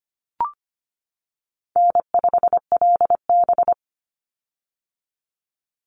Morse code, heard as a keyed tone, about 25 wpm, 700 Hz. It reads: N5LB
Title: Call Signs - 25wpm